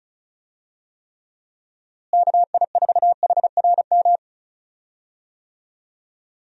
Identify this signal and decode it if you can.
Morse code, heard as a keyed tone, about 34 wpm, 700 Hz. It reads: KI4HRM
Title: USA Call Signs - 6 Characters - Repeated 3x - 35wpm